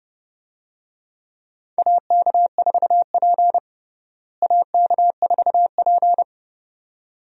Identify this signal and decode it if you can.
Morse code, heard as a keyed tone, about 30 wpm, 700 Hz. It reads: AK4P AK4P
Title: USA Call Signs - 4 Characters - Repeated 2x - 30wpm